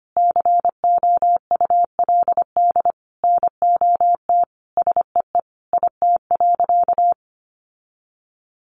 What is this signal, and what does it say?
Morse code, heard as a keyed tone, about 25 wpm, 700 Hz. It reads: COULD NOT SEE IT.
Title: Sentences from Top 1000 Words 25wpm